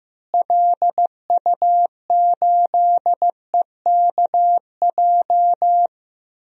Morse code, heard as a keyed tone, about 15 wpm, 700 Hz. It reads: LU8EKJ
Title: DXCC-100 Call Signs - 6 Characters - Repeated 3x - 15wpm